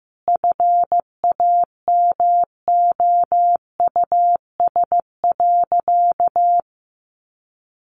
Morse code, heard as a keyed tone, about 15 wpm, 700 Hz. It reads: FAMOUS.
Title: Sentences from Top 1500 Words 15wpm